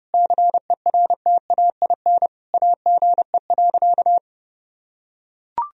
Morse code, heard as a keyed tone, about 32 wpm, 700 Hz. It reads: CERTAIN AGE.
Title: Sentences from Top 1000 Words - Review Part 3 - 30wpm